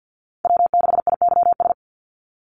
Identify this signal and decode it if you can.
Morse code, heard as a keyed tone, about 50 wpm, 700 Hz. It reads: W6IYS
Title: NA SOTA Chaser Callsigns 50wpm